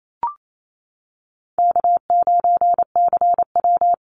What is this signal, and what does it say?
Morse code, heard as a keyed tone, about 28 wpm, 700 Hz. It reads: K9CW
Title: Easy Call Signs - 28wpm